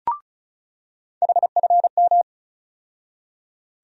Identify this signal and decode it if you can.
Morse code, heard as a keyed tone, about 36 wpm, 700 Hz. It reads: HFM
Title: USA Call Sign Postfixes - 3 Letters 35wpm